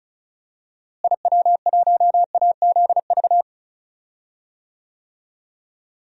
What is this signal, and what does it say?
Morse code, heard as a keyed tone, about 36 wpm, 700 Hz. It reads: IW1AZV
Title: DXCC-100 Call Signs - 6 Characters - Repeated 3x - 35wpm